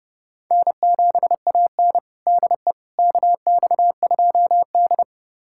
Morse code, heard as a keyed tone, about 32 wpm, 700 Hz. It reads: N7AN DE KX2D
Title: USA Callsign DE Callsign - 4 Characters - 30wpm